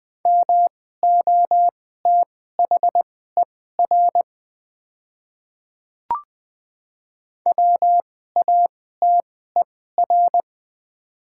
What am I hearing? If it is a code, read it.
Morse code, heard as a keyed tone, about 21 wpm, spaced as for 15 wpm, 700 Hz. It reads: MOTHER WATER
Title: 2x Character Spacing - Top 200 Words - 20wpm